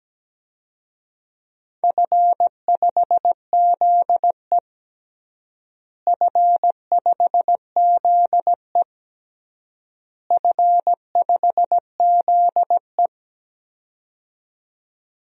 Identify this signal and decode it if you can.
Morse code, heard as a keyed tone, about 17 wpm, 700 Hz. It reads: F5ZE F5ZE F5ZE
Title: DXCC-100 Call Signs - 4 Characters - Repeated 3x - 17wpm